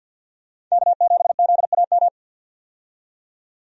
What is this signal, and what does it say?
Morse code, heard as a keyed tone, about 53 wpm, 700 Hz. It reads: K7ZAM
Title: USA Call Signs - 5 Characters - 50wpm